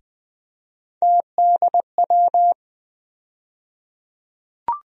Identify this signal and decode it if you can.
Morse code, heard as a keyed tone, about 21 wpm, 700 Hz. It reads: TDW
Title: USA Call Sign Postfixes - 3 Letters 20wpm